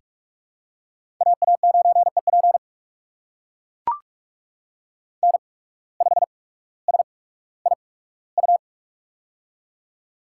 Morse code, heard as a keyed tone, about 44 wpm, spaced as for 19 wpm, 700 Hz. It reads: AA9EP N5SIU
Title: USA Call Signs - 5 Characters - 8x Character Spacing - 45wpm